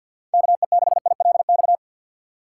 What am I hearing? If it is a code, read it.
Morse code, heard as a keyed tone, about 51 wpm, 700 Hz. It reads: KE6ILX
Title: USA Call Signs - 3x Character Spacing - 50wpm